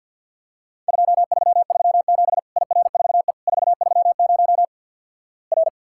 Morse code, heard as a keyed tone, about 52 wpm, 700 Hz. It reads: 2337 IR4E 430 R
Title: Worked All Europe DX Contest - QTC Exchange with Repeats - 50wpm